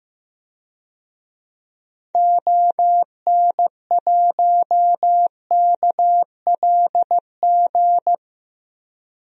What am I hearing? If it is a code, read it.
Morse code, heard as a keyed tone, about 16 wpm, 700 Hz. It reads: ON1KLG
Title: DXCC-100 Call Signs - 6 Characters - Repeated 3x - 15wpm